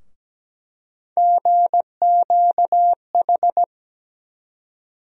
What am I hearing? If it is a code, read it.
Morse code, heard as a keyed tone, about 17 wpm, 700 Hz. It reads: GQH
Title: USA Call Sign Postfixes - 3 Letters 17wpm